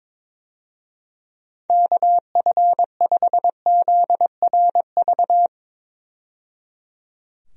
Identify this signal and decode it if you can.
Morse code, heard as a keyed tone, about 22 wpm, 700 Hz. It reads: KF5ZRV
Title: USA Call Signs - 6 Characters - 22wpm